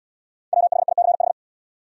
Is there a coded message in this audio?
Yes: Morse code from 0.5 to 1.3 s, heard as a keyed tone, about 69 wpm, 700 Hz. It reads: WHICH